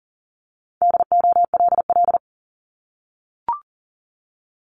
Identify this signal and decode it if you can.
Morse code, heard as a keyed tone, about 37 wpm, 700 Hz. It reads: DOLL
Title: Multi-letter Phonemes Repeated 3x 40wpm